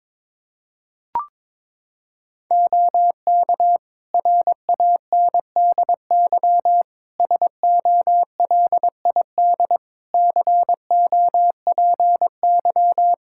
Morse code, heard as a keyed tone, about 22 wpm, 700 Hz. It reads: OK RANDY SOLID COPY
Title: Practice QSO - Exchange 3 - More Info - 22wpm